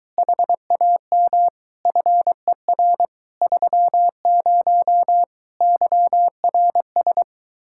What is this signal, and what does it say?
Morse code, heard as a keyed tone, about 23 wpm, 700 Hz. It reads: HAM FER 30 YRS